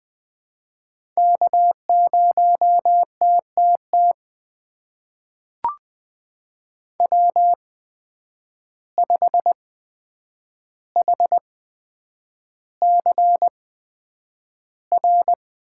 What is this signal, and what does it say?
Morse code, heard as a keyed tone, about 20 wpm, spaced as for 9 wpm, 700 Hz. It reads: K0TTT W5HCR
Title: USA Call Signs - 5 Characters - 8x Character Spacing - 20wpm